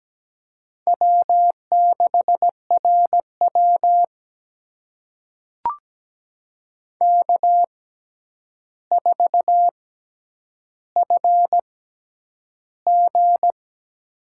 Morse code, heard as a keyed tone, about 17 wpm, spaced as for 9 wpm, 700 Hz. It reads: W6RW K4FG
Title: USA Call Signs - 4 Characters - 6x Character Spacing - 17wpm